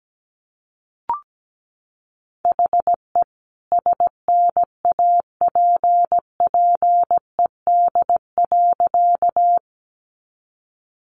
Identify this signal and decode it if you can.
Morse code, heard as a keyed tone, about 17 wpm, 700 Hz. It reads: HE SNAPPED.
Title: A Sound Of Thunder 17wpm